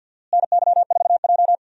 Morse code, heard as a keyed tone, about 47 wpm, 700 Hz. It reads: NY4J
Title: USA Call Signs - 4 Characters - Repeated 3x - 50wpm